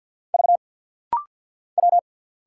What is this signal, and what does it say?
Morse code, heard as a keyed tone, about 50 wpm, 690 Hz. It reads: V W
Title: Single Letters - Rapid-Fire 50wpm